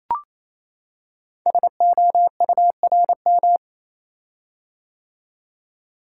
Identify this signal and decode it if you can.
Morse code, heard as a keyed tone, about 28 wpm, 700 Hz. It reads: SOURM
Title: Code Groups - 5 Letters - 28wpm